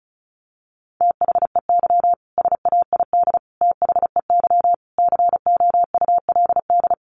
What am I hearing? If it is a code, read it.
Morse code, heard as a keyed tone, about 34 wpm, 700 Hz. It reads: THEY SAID THEY COULD
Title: Sentences from Top 1400 Words 35wpm